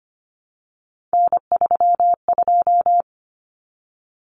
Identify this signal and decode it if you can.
Morse code, heard as a keyed tone, about 25 wpm, 700 Hz. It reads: N32
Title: Common Cut Numbers - 3-Digit 25wpm